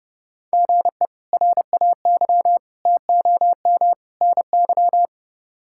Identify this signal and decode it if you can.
Morse code, heard as a keyed tone, about 31 wpm, 700 Hz. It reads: GE RAY TOM NY
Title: K1USN SST QSO Practice - 30wpm